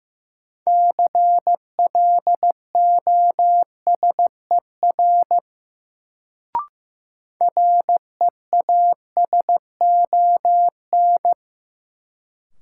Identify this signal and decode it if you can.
Morse code, heard as a keyed tone, about 15 wpm, 700 Hz. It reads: CLOSER REASON
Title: Sets of 5 Words Encore - New Word Drill - 15wpm